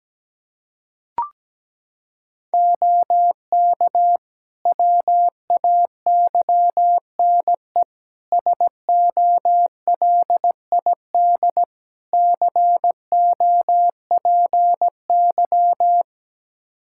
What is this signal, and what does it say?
Morse code, heard as a keyed tone, about 17 wpm, 700 Hz. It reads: OK WAYNE SOLID COPY
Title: Practice QSO - Exchange 3 - More Info - 17wpm